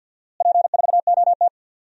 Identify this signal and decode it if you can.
Morse code, heard as a keyed tone, about 47 wpm, 700 Hz. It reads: P4OT
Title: DXCC-100 Call Signs - 4 Characters - Repeated 2x - 50wpm